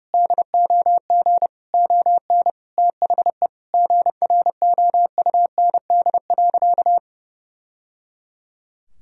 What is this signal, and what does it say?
Morse code, heard as a keyed tone, about 28 wpm, 700 Hz. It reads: DOG ON THE GROUND.
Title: Sentences from Top 1000 Words - Review Part 2 - 30wpm